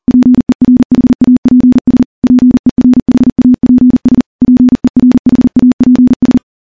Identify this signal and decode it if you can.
Morse code, heard as a keyed tone, about 40 wpm, 250 Hz. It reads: PERHAPS PERHAPS PERHAPS
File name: morse-training-40WPM